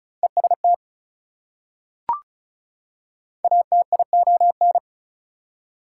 Morse code, heard as a keyed tone, about 36 wpm, 700 Hz. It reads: EST ATION
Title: Prefixes and Suffixes 35wpm